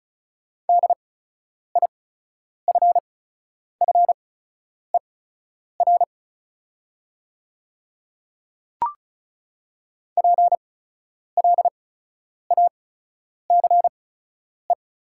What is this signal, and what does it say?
Morse code, heard as a keyed tone, about 36 wpm, spaced as for 10 wpm, 690 Hz. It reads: DIFFER PLACE
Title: 8x Character Spacing - Top 300 Words - 35wpm